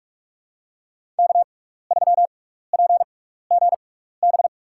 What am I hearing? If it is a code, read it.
Morse code, heard as a keyed tone, about 46 wpm, spaced as for 15 wpm, 700 Hz. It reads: K3PGB
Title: USA Call Signs - 5 Characters - 6x Character Spacing - 45wpm